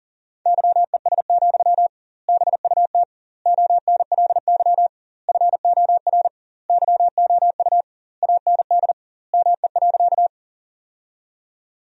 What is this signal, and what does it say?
Morse code, heard as a keyed tone, about 40 wpm, 700 Hz. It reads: YES, BUT ONLY FOR YOU AND ME.
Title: 2x Word-Spacing - Sentences from Top 500 Words - 40wpm